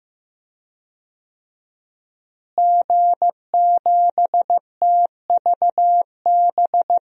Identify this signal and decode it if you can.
Morse code, heard as a keyed tone, about 15 wpm, 700 Hz. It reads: G7TVB
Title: DXCC-100 Call Signs - 5 Characters - Repeated 3x - 15wpm